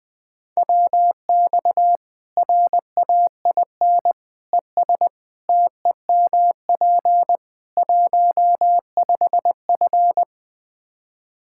Morse code, heard as a keyed tone, about 20 wpm, 700 Hz. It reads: WX RAIN ES TEMP 15F
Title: Practice QSO - Exchange 2 - Info - 20wpm